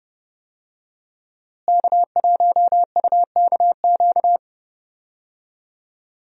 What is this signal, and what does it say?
Morse code, heard as a keyed tone, about 30 wpm, 700 Hz. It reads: K1UKQ
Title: USA Call Signs - 5 Characters - Repeated 2x - 30wpm